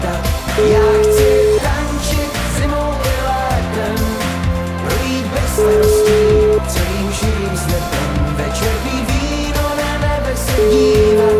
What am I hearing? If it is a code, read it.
Ringback tone (Europe, 425 Hz), with noise.